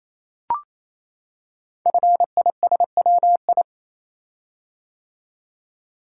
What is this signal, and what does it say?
Morse code, heard as a keyed tone, about 29 wpm, 700 Hz. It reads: FISWI
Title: Code Groups - 5 Letters - 28wpm